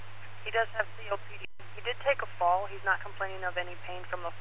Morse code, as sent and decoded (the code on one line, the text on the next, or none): none